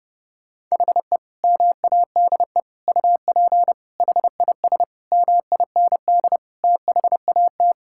Morse code, heard as a keyed tone, about 30 wpm, 700 Hz. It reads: HE MADE UP HIS MIND THAT
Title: Sentences from Top 1600 Words 30wpm